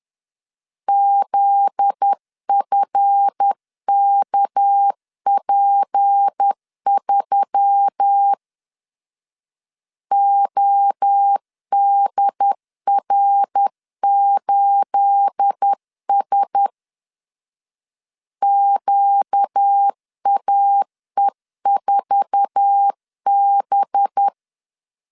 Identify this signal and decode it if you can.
Morse code, heard as a keyed tone, about 10 wpm, 800 Hz. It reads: ZFKP3 ODR8S QAE4B
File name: LEZIONI / 9 28 10wpm 2842